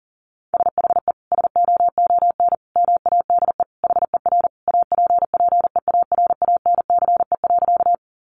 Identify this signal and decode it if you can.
Morse code, heard as a keyed tone, about 40 wpm, 700 Hz. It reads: SHE SOON MADE HER APPEARANCE.
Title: Sentences from Top 1100 Words 40wpm